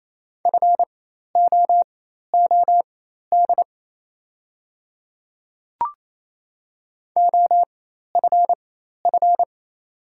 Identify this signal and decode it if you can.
Morse code, heard as a keyed tone, about 28 wpm, spaced as for 13 wpm, 690 Hz. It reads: FOOD OFF
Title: 4x Character Spacing - Top 500 Words - 28wpm